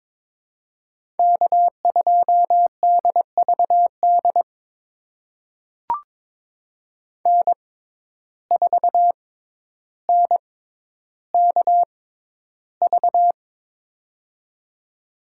Morse code, heard as a keyed tone, about 22 wpm, spaced as for 11 wpm, 700 Hz. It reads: K2DVD N4NKV
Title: USA Call Signs - 5 Characters - 6x Character Spacing - 22wpm